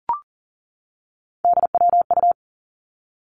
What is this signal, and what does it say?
Morse code, heard as a keyed tone, about 38 wpm, 700 Hz. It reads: DWU
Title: Three Challenging Letter-Number Combinations 40wpm